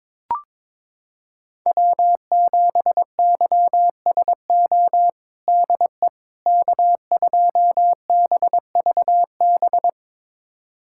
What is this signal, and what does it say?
Morse code, heard as a keyed tone, about 22 wpm, 700 Hz. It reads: W7YSO DE K2BVB
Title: USA Callsign DE Callsign - 5 Characters - 22wpm